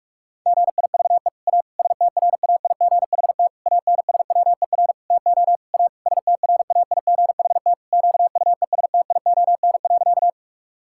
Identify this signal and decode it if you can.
Morse code, heard as a keyed tone, about 42 wpm, 700 Hz. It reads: GIVE A STRAIGHT ANSWER TO A STRAIGHT QUESTION.
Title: Sentences from Top 1000 Words - Review Part 6 - 45wpm